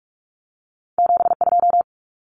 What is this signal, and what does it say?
Morse code, heard as a keyed tone, about 46 wpm, 700 Hz. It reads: 72
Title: Common Cut Numbers - 2-Digit 45wpm